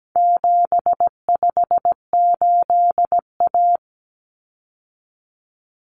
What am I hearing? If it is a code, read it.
Morse code, heard as a keyed tone, about 17 wpm, 700 Hz. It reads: 758A
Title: Common Cut Numbers - 4-Digit 17wpm